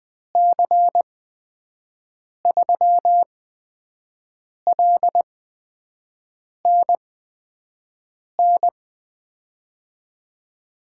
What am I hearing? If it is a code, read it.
Morse code, heard as a keyed tone, about 20 wpm, spaced as for 5 wpm, 700 Hz. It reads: C3LNN
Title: USA Call Signs - 6 Characters - 8x Character Spacing - 20wpm